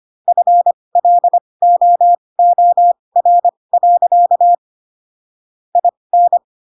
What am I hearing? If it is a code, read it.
Morse code, heard as a keyed tone, about 25 wpm, spaced as for 20 wpm, 700 Hz. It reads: FLOOR. IN
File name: Pinocchio / Pinocchio ch7 15 25 ik1hge